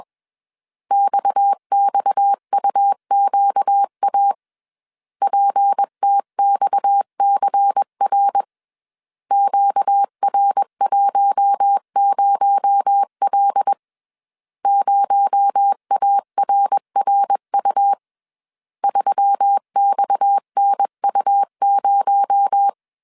Morse code, heard as a keyed tone, about 21 wpm, 800 Hz. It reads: XXUQA PTXCR QR10L 0ARRU 3XNU0
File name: CW / DET / 1 6 1500C S B 20wpm 2048